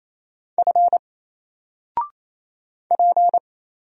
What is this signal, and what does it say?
Morse code, heard as a keyed tone, about 27 wpm, 700 Hz. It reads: F P
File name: Single-Letter-Number-28wpm